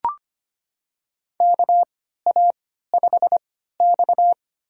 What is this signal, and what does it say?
Morse code, heard as a keyed tone, about 25 wpm, spaced as for 14 wpm, 700 Hz. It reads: KA5X
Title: USA Call Signs - 4 Characters - 3x Character Spacing - 25wpm